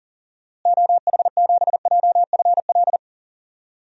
Morse code, heard as a keyed tone, about 40 wpm, 700 Hz. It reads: OH7JFL
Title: DXCC-100 Call Signs - 6 Characters - Repeated 3x - 40wpm